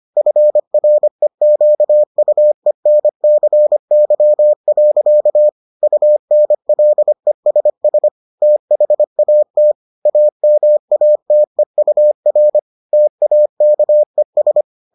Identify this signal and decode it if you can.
Morse code, heard as a keyed tone, about 25 wpm, 600 Hz. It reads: FREQUENCY. UNLESS THAT AMATEUR TAKES